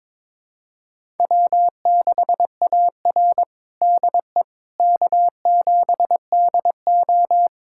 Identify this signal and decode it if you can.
Morse code, heard as a keyed tone, about 22 wpm, 700 Hz. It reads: W6AR DE K7DO
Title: USA Callsign DE Callsign - 4 Characters - 22wpm